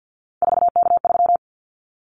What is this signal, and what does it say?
Morse code, heard as a keyed tone, about 50 wpm, 700 Hz. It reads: 4K3
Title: Callsign Prefixes for Top 100 DXCC - 50wpm